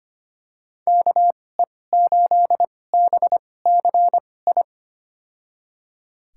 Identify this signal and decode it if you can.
Morse code, heard as a keyed tone, about 25 wpm, spaced as for 18 wpm, 700 Hz. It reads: KE8BCI
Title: USA Call Signs - 6 Characters - 2x Character Spacing - 25wpm